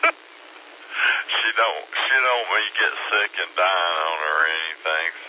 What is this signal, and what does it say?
Emergency Alert System (SAME) tones, heard as harsh buzzing bursts, with speech. It is heard from 0.8 to 4.6 s.